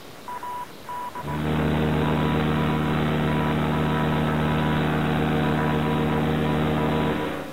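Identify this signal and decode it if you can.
Morse code, heard as a keyed tone, about 17 wpm, 980 Hz. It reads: AD ASTRA PER
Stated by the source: Morse code from the Voyager Golden Record, sent into space aboard NASA's Voyager probes.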